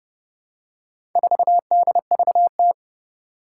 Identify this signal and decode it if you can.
Morse code, heard as a keyed tone, about 30 wpm, 690 Hz. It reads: 4DVT
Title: Four Challenging Letter-Number Combinations 30wpm